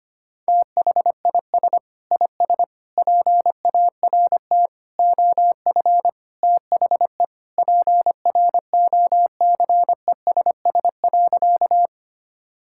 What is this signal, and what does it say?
Morse code, heard as a keyed tone, about 25 wpm, 700 Hz. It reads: THIS IS PART OF THE PROCESS.
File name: practice / 25wpm / Sentences with 2-Word Phrases - 25wpm